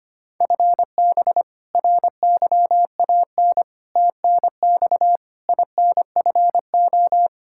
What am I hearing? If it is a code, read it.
Morse code, heard as a keyed tone, about 25 wpm, 700 Hz. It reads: FB RYAN TNX INFO
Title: K1USN SST QSO Practice 25wpm